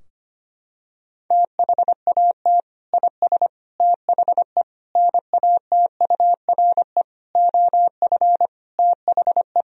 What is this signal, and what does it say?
Morse code, heard as a keyed tone, about 25 wpm, 700 Hz. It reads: THAT IS THE NATURE OF THE
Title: Sentences from Top 1000 Words - Review Part 4 - 25wpm